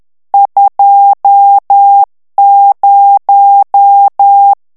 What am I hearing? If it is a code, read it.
Morse code, heard as a keyed tone, about 10 wpm, 800 Hz. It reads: 20
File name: morse mp3 / ITALIEN texte clair 1 vitesse 10 mots minute